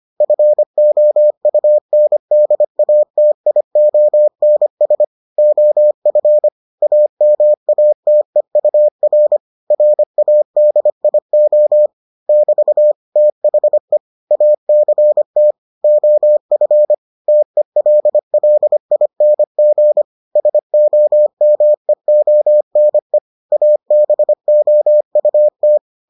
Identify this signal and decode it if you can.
Morse code, heard as a keyed tone, about 25 wpm, 600 Hz. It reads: FOUNDATIONS OF AMATEUR RADIO <BT> THE ACT OF TELLING SOMEONE ABOUT